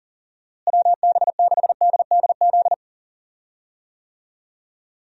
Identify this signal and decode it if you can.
Morse code, heard as a keyed tone, about 40 wpm, 700 Hz. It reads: WB6DDZ
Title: USA Call Signs - 6 Characters - Repeated 3x - 40wpm